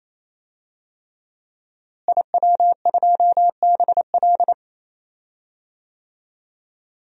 Morse code, heard as a keyed tone, about 28 wpm, 700 Hz. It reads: IW2BL
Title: DXCC-100 Call Signs - 5 Characters - Repeated 3x - 28wpm